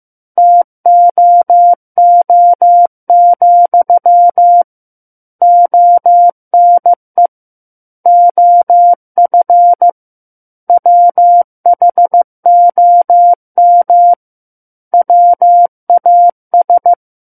Morse code, heard as a keyed tone, about 15 wpm, 700 Hz. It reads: TOO, ONE OF WHOM WAS
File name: Book One / 06-Chapter 6 - 15 WPM